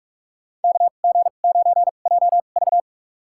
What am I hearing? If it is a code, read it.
Morse code, heard as a keyed tone, about 46 wpm, spaced as for 33 wpm, 700 Hz. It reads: KG9JV
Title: USA Call Signs - 5 Characters - 2x Character Spacing - 45wpm